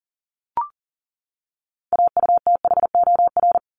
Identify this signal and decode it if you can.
Morse code, heard as a keyed tone, about 40 wpm, 700 Hz. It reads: AUTHOR
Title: Top 1001-1250 Words 40wpm